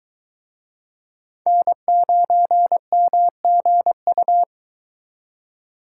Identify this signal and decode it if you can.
Morse code, heard as a keyed tone, about 23 wpm, 700 Hz. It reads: N9MGU